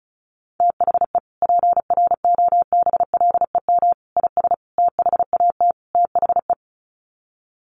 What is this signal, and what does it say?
Morse code, heard as a keyed tone, about 34 wpm, 700 Hz. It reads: THE PROBLEM IS THAT THE
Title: Sets of 5 Words Encore2 35wpm